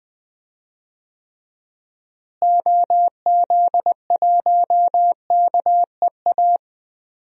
Morse code, heard as a keyed tone, about 20 wpm, 700 Hz. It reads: OZ1KEA
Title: DXCC-100 Call Signs - 6 Characters - Repeated 2x - 20wpm